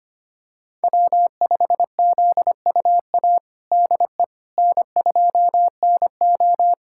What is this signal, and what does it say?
Morse code, heard as a keyed tone, about 25 wpm, 700 Hz. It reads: W5ZUA DE N2NO
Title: USA Callsign DE Callsign 25wpm